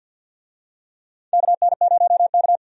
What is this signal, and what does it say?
Morse code, heard as a keyed tone, about 50 wpm, 700 Hz. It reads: KN0K
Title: USA Call Signs - 4 Characters - 50wpm